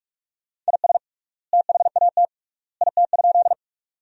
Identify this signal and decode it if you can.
Morse code, heard as a keyed tone, about 46 wpm, 700 Hz. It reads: IS THAT IT?
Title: 3x Word-Spacing - Sentences from Top 100 Words - 45wpm